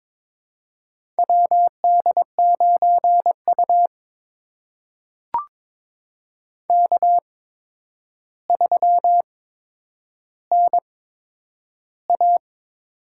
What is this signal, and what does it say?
Morse code, heard as a keyed tone, about 22 wpm, spaced as for 9 wpm, 700 Hz. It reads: WD9U K3NA
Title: USA Call Signs - 4 Characters - 8x Character Spacing - 22wpm